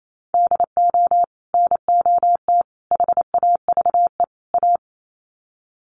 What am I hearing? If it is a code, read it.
Morse code, heard as a keyed tone, about 28 wpm, 700 Hz. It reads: DO NOT HAVE A
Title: Sets of 4 Words 28wpm